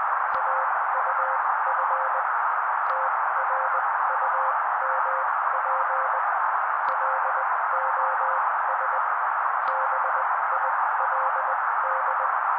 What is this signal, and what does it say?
Morse code, heard as a keyed tone, about 20 wpm, spaced as for 16 wpm, 530 Hz. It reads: AUF TRUMP LOS BILD
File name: daily / 16wpm noise